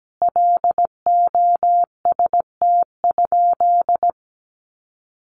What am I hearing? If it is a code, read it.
Morse code, heard as a keyed tone, about 17 wpm, 700 Hz. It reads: LOST?
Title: Sentences from Top 1400 Words 17wpm